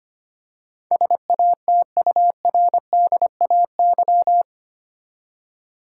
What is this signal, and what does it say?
Morse code, heard as a keyed tone, about 25 wpm, 700 Hz. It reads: SATURDAY
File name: 8 Letter Words - First 3 Letters Given - 25wpm